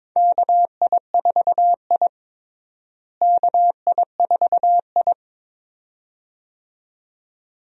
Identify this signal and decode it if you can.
Morse code, heard as a keyed tone, about 22 wpm, 700 Hz. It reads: KI4I KI4I
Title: USA Call Signs - 4 Characters - Repeated 2x - 22wpm